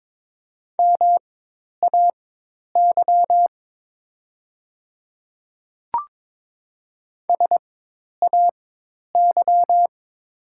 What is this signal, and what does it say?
Morse code, heard as a keyed tone, about 22 wpm, spaced as for 10 wpm, 700 Hz. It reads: MAY SAY
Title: 4x Character Spacing - Top 500 Words - 22wpm